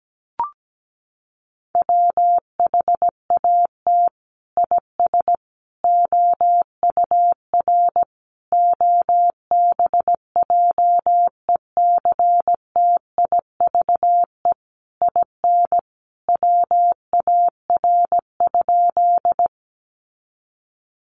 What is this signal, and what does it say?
Morse code, heard as a keyed tone, about 17 wpm, 700 Hz. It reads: WHAT IS OUR OBJECTIVE IN WAR?
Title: Sentences from Top 1200 Words 17wpm